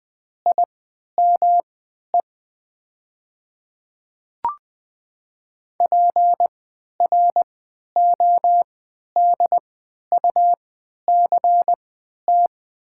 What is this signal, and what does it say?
Morse code, heard as a keyed tone, about 20 wpm, spaced as for 11 wpm, 700 Hz. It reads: IME PRODUCT
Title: 3x Character Spacing - Top 500 Words - 20wpm